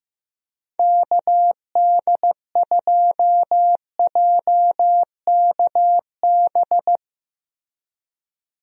Morse code, heard as a keyed tone, about 16 wpm, 700 Hz. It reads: KD2JKB
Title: USA Call Signs - 6 Characters - Repeated 2x - 15wpm